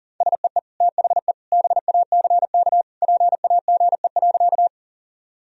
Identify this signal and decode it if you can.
Morse code, heard as a keyed tone, about 40 wpm, 700 Hz. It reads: SEE THE BACK PAGE.
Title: Sentences from Top 1000 Words - Review Part 1 - 40wpm